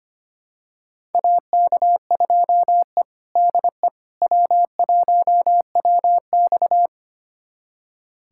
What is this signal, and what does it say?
Morse code, heard as a keyed tone, about 25 wpm, 700 Hz. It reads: AK2E DE W1WX
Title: USA Callsign DE Callsign - 4 Characters - 25wpm